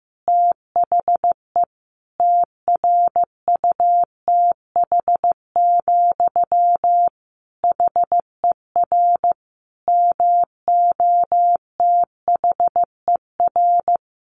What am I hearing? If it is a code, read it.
Morse code, heard as a keyed tone, about 15 wpm, 700 Hz. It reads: THE TRUTH, HER MOTHER
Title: Sentences from Top 800 Words 15wpm